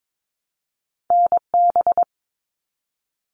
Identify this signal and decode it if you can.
Morse code, heard as a keyed tone, about 22 wpm, 700 Hz. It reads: NB
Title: Canadian Provinces and Territories 22wpm